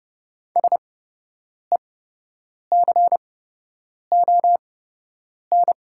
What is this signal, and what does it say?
Morse code, heard as a keyed tone, about 30 wpm, spaced as for 8 wpm, 690 Hz. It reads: SECON
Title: 8x Character Spacing - Top 500 Words - 30wpm